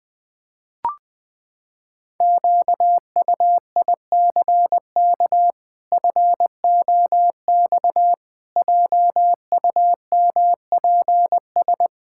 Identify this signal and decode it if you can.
Morse code, heard as a keyed tone, about 20 wpm, 700 Hz. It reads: QUICK FOX JUMPS
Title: Pangram Challenge v2 - 20wpm